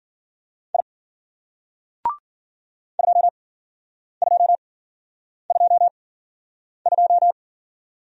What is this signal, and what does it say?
Morse code, heard as a keyed tone, about 51 wpm, spaced as for 9 wpm, 700 Hz. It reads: I 2222